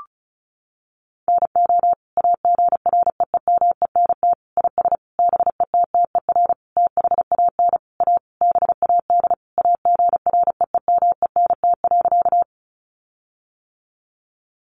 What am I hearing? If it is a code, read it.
Morse code, heard as a keyed tone, about 34 wpm, 700 Hz. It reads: NO AGREEMENT IS BETTER THAN A BAD AGREEMENT.
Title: Sentences from Top 1500 Words 35wpm